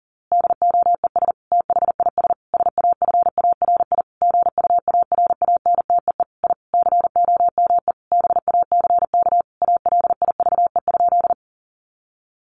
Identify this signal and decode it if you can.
Morse code, heard as a keyed tone, about 40 wpm, 700 Hz. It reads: DOES THIS SAFARI GUARANTEE I COME BACK ALIVE?
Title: A Sound Of Thunder 40wpm